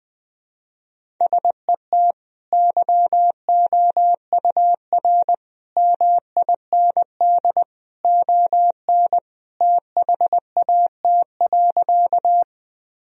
Morse code, heard as a keyed tone, about 19 wpm, 700 Hz. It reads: SET YOUR MIND ON THAT.
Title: Sentences from Top 1000 Words - Review Part 1 - 20wpm